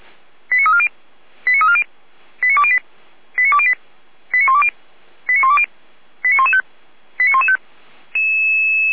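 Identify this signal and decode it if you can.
Five-tone selective call (ZVEI), heard as quick runs of five tones, tones of about 70 ms. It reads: 89340 89340 89208 89208 89120 89120 89115 89115
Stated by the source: ZVEI five-tone selective call 89094, received by SDR.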